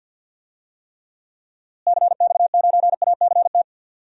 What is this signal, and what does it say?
Morse code, heard as a keyed tone, about 46 wpm, 700 Hz. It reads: CX9ACT
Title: DXCC-100 Call Signs - 6 Characters - Repeated 2x - 50wpm